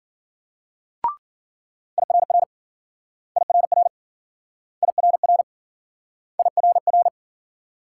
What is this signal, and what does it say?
Morse code, heard as a keyed tone, about 55 wpm, 700 Hz. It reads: IRR IRR IRR IRR